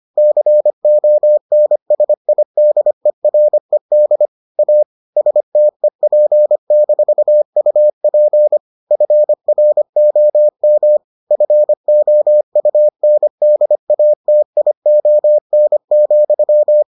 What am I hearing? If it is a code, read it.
Morse code, heard as a keyed tone, about 25 wpm, 600 Hz. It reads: CONSIDERED A STEP-UP FROM FOUNDATION,